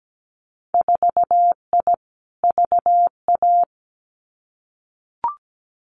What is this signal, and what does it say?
Morse code, heard as a keyed tone, about 17 wpm, 700 Hz. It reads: 4I VA
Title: Winter Field Day Exchanges 17wpm